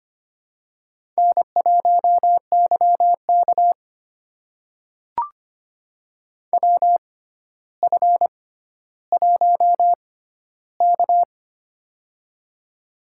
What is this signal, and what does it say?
Morse code, heard as a keyed tone, about 25 wpm, spaced as for 13 wpm, 700 Hz. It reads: N1YK WF1K
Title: USA Call Signs - 4 Characters - 6x Character Spacing - 25wpm